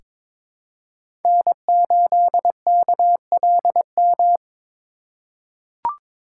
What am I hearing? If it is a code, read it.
Morse code, heard as a keyed tone, about 21 wpm, 700 Hz. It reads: N8KLM
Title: USA Call Signs - 5 Characters - 4x Character Spacing - 22wpm